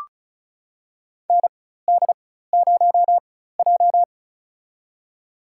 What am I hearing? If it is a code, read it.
Morse code, heard as a keyed tone, about 36 wpm, spaced as for 16 wpm, 700 Hz. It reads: ND0J
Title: USA Call Signs - 4 Characters - 4x Character Spacing - 35wpm